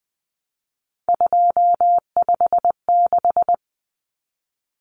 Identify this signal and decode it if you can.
Morse code, heard as a keyed tone, about 20 wpm, 700 Hz. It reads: 256
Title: Common Cut Numbers - 3-Digit 20wpm